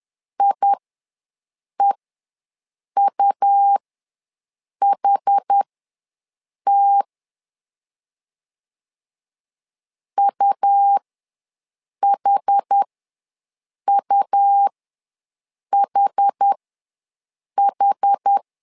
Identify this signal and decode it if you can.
Morse code, heard as a keyed tone, about 10 wpm, spaced as for 6 wpm, 800 Hz. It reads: IEUHT UHUHH
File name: LEZIONI / 2 UHDN / 2 UH ws 10wpm 2029